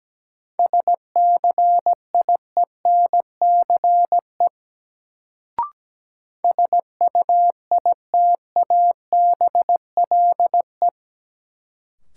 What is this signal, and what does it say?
Morse code, heard as a keyed tone, about 17 wpm, 700 Hz. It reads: SCIENCE SUITABLE
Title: Sentences from Top 1400 Words - New Word Drill Alt - 17wpm